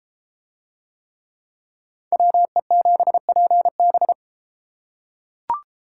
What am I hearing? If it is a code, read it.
Morse code, heard as a keyed tone, about 32 wpm, 700 Hz. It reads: WE7PB